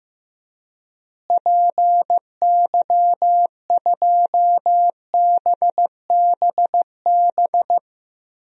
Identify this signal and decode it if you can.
Morse code, heard as a keyed tone, about 15 wpm, 700 Hz. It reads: PY2BBB
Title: DXCC-100 Call Signs - 6 Characters - Repeated 2x - 15wpm